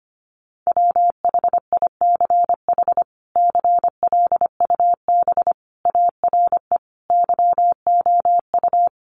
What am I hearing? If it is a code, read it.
Morse code, heard as a keyed tone, about 25 wpm, 700 Hz. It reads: WHICH CLUB ARE YOU
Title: Sentences from Top 1200 Words 25wpm